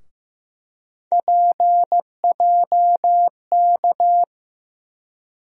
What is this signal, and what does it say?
Morse code, heard as a keyed tone, about 15 wpm, 700 Hz. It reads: PJK
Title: USA Call Sign Postfixes - 3 Letters 15wpm